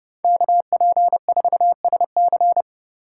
Morse code, heard as a keyed tone, about 30 wpm, 700 Hz. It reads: KP4SC